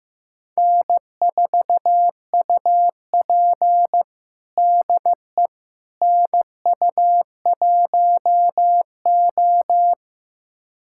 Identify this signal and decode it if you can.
Morse code, heard as a keyed tone, about 15 wpm, 700 Hz. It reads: N4UP DE NU1O
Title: USA Callsign DE Callsign - 4 Characters - 15wpm